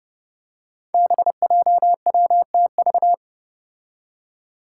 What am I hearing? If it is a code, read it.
Morse code, heard as a keyed tone, about 30 wpm, 700 Hz. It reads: BJWTV
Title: Five Challenging Letter-Number Combinations 30wpm